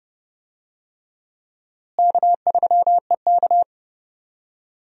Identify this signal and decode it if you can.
Morse code, heard as a keyed tone, about 30 wpm, 700 Hz. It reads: K3EK
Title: USA Call Signs - 4 Characters - Repeated 3x - 30wpm